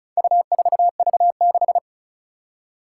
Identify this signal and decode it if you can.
Morse code, heard as a keyed tone, about 36 wpm, 700 Hz. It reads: U4V6
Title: Four Challenging Letter-Number Combinations 35wpm